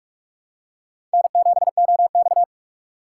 Morse code, heard as a keyed tone, about 44 wpm, 690 Hz. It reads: N7OX
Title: USA Call Signs - 4x Character Spacing - 45wpm